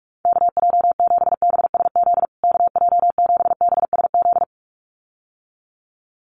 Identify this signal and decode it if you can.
Morse code, heard as a keyed tone, about 44 wpm, 700 Hz. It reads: KJ7BSZ KJ7BSZ
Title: USA Call Signs Repeated 2x 45wpm